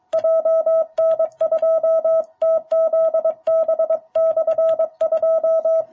Morse code, heard as a keyed tone, about 23 wpm, 660 Hz, with radio noise. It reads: JN2TZB/2
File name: jh8cbh / 2021cw